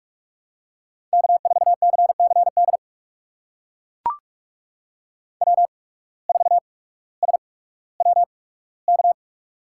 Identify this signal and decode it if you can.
Morse code, heard as a keyed tone, about 44 wpm, spaced as for 19 wpm, 700 Hz. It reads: K4CCD W4SWK
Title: USA Call Signs - 5 Characters - 8x Character Spacing - 45wpm